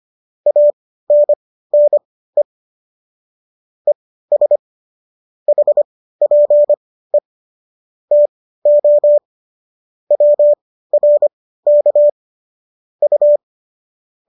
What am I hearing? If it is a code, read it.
Morse code, heard as a keyed tone, about 25 wpm, spaced as for 15 wpm, 590 Hz. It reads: ANNE ES HPE TO WRK U